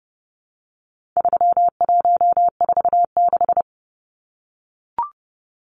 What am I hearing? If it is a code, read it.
Morse code, heard as a keyed tone, about 29 wpm, 700 Hz. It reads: 3146
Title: Numbers - 4-Digit 30wpm